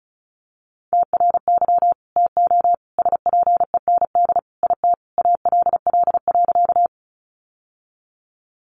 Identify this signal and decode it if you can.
Morse code, heard as a keyed tone, about 34 wpm, 700 Hz. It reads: TRY TO SPEND IT ALL.
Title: Sentences from Top 800 Words 35wpm